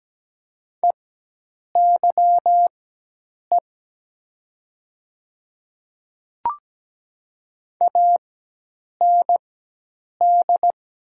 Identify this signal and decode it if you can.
Morse code, heard as a keyed tone, about 17 wpm, spaced as for 8 wpm, 700 Hz. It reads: EYE AND
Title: 4x Character Spacing - Top 300 Words - 17wpm